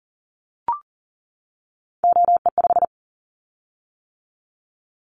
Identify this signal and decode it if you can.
Morse code, heard as a keyed tone, about 42 wpm, 700 Hz. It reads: OE5
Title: Callsign Prefixes for Top 25 DXCC - 40wpm